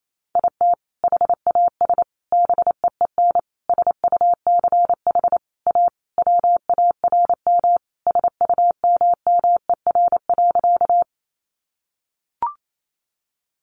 Morse code, heard as a keyed tone, about 28 wpm, 700 Hz. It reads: IT HAS BEEN SUCH A WARM SUMMER.
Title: Sentences from Top 1100 Words 28wpm